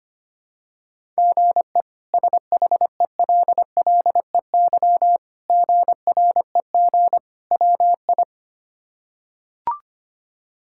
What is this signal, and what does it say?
Morse code, heard as a keyed tone, about 25 wpm, 700 Hz. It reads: GE SHELLEY GREG WI
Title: K1USN SST QSO Practice - 25wpm